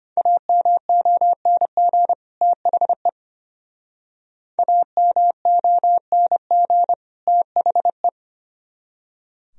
Morse code, heard as a keyed tone, about 30 wpm, 700 Hz. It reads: AMONG THE AMONG THE
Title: Speed-Racing - Sets of 2 Words 25wpm